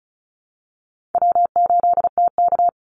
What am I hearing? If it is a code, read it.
Morse code, heard as a keyed tone, about 34 wpm, 700 Hz. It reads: W8TK
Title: Call Signs Repeated 3x - 35wpm